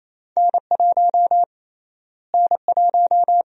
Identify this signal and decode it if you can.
Morse code, heard as a keyed tone, about 28 wpm, 700 Hz. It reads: N1 N1
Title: USA Call Sign Prefixes - Repeated 2x - 28wpm